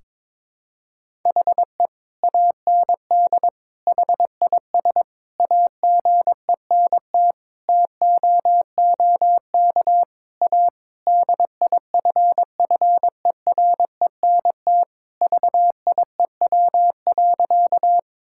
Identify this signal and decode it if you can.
Morse code, heard as a keyed tone, about 22 wpm, 700 Hz. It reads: HE AND HIS AGENT TOOK A DIFFERENT VIEW.
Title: Sentences from Top 1600 Words 22wpm